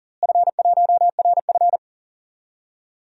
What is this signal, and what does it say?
Morse code, heard as a keyed tone, about 40 wpm, 700 Hz. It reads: F1RF
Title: DXCC-100 Call Signs - 4 Characters - Repeated 2x - 40wpm